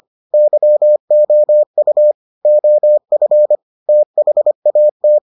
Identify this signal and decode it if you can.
Morse code, heard as a keyed tone, about 25 wpm, 600 Hz. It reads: YOU OF THAT